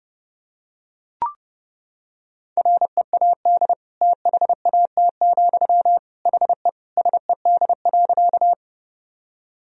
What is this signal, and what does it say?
Morse code, heard as a keyed tone, about 32 wpm, 700 Hz. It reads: READ THAT, HE SED.
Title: Sentences from Top 200 Words and Abbreviations 30wpm